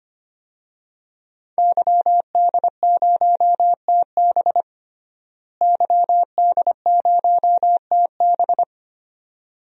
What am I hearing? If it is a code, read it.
Morse code, heard as a keyed tone, about 25 wpm, 700 Hz. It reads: YD0TB YD0TB
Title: DXCC-100 Call Signs - 5 Characters - Repeated 2x - 25wpm